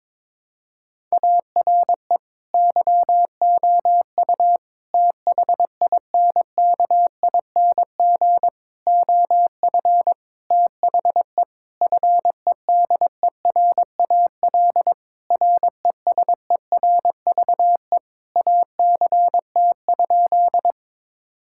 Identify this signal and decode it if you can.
Morse code, heard as a keyed tone, about 22 wpm, 700 Hz. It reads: ARE YOU THINKING OF THE FEDERAL RESERVE ACT?
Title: Sentences from Top 1000 Words - Review Part 6 - 22wpm